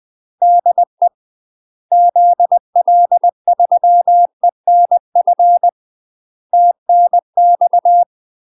Morse code, heard as a keyed tone, about 20 wpm, 700 Hz. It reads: DE ZL3ENF TNX
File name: UK MASTER QSOs TRAINING / 20WPM / UK MASTER QSOs 0031